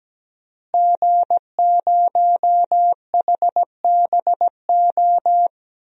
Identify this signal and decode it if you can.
Morse code, heard as a keyed tone, about 17 wpm, 700 Hz. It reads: G0HBO
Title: DXCC-100 Call Signs - 5 Characters - Repeated 3x - 17wpm